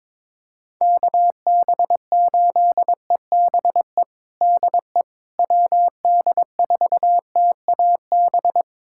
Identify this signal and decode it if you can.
Morse code, heard as a keyed tone, about 22 wpm, 700 Hz. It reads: KB8EBE DE WD4TAB
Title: USA Callsign DE Callsign - 6 Characters - 22wpm